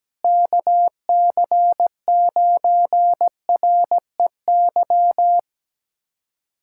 Morse code, heard as a keyed tone, about 17 wpm, 700 Hz. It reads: KC9REY
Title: USA Callsign DE Callsign - 6 Characters - 17wpm